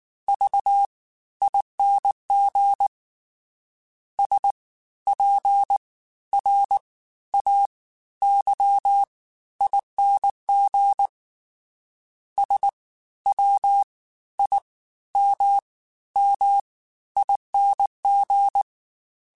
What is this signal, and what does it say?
Morse code, heard as a keyed tone, about 19 wpm, spaced as for 13 wpm, 800 Hz. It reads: VING SPRAYING SWIMMING
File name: Special INGs-mp3s / Special INGs-mp3s / Fast ing @18